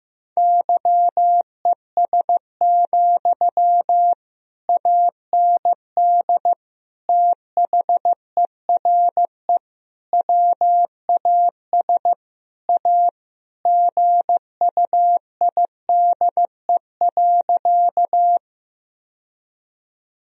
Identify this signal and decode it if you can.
Morse code, heard as a keyed tone, about 15 wpm, 700 Hz. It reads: YES, AND THERE WAS A GUIDE.
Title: Sentences from Top 1000 Words - Review Part 3 - 15wpm